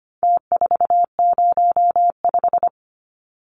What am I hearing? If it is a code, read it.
Morse code, heard as a keyed tone, about 25 wpm, 700 Hz. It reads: T405
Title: Numbers - Contest Style - 4-Digit 25wpm